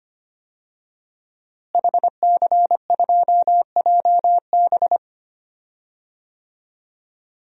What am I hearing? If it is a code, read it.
Morse code, heard as a keyed tone, about 25 wpm, 700 Hz. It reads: HC2JB
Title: DXCC-100 Call Signs - 5 Characters - Repeated 2x - 25wpm